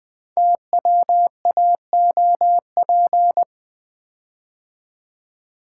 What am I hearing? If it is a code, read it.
Morse code, heard as a keyed tone, about 20 wpm, 700 Hz. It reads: TWAOP
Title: Code Groups - 5 Letters - 20wpm